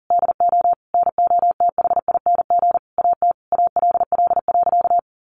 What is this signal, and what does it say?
Morse code, heard as a keyed tone, about 40 wpm, 700 Hz. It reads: DO NOTHING AT ALL.
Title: Sentences from Top 1300 Words 40wpm